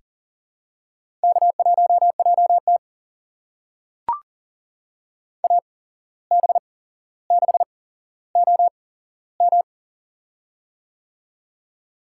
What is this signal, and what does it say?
Morse code, heard as a keyed tone, about 39 wpm, spaced as for 16 wpm, 700 Hz. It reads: K1JT AB6OM
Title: USA Call Signs - 8x Character Spacing - 40wpm